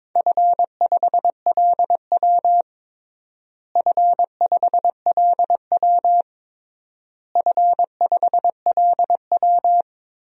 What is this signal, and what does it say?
Morse code, heard as a keyed tone, about 22 wpm, 700 Hz. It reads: F5LW F5LW F5LW
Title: DXCC-100 Call Signs - 4 Characters - Repeated 3x - 22wpm